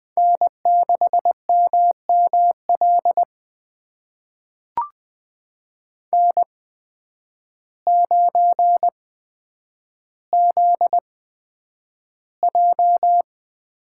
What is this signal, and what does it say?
Morse code, heard as a keyed tone, about 20 wpm, spaced as for 9 wpm, 700 Hz. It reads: N6MML N9ZJ
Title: USA Call Signs - 5 Characters - 8x Character Spacing - 20wpm